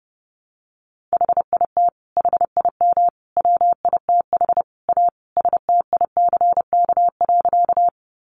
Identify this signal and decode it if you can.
Morse code, heard as a keyed tone, about 30 wpm, 700 Hz. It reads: HIT HIM WITH A STICK.
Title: Sentences from Top 1200 Words 30wpm